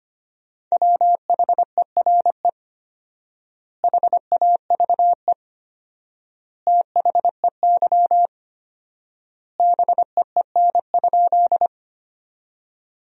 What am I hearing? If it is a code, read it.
Morse code, heard as a keyed tone, about 25 wpm, 700 Hz. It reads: WHERE HAVE THEY BEEN?
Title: 4x Word-Spacing - Sentences from Top 200 Words - 25wpm